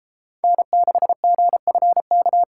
Morse code, heard as a keyed tone, about 37 wpm, 700 Hz. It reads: N6GFK